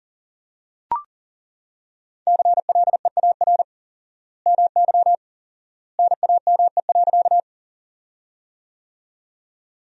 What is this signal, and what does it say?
Morse code, heard as a keyed tone, about 40 wpm, 700 Hz. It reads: CLEAR MY NAME.
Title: 4x Word-Spacing - Sentences from Top 500 Words - 40wpm